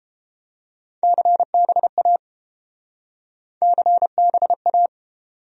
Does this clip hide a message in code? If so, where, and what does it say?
Yes: Morse code from 1.0 to 4.9 s, heard as a keyed tone, about 32 wpm, 700 Hz. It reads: CBA CBA